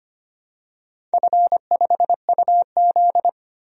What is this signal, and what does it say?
Morse code, heard as a keyed tone, about 25 wpm, 700 Hz. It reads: F5UZ
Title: DXCC-100 Call Signs - 4 Characters - 25wpm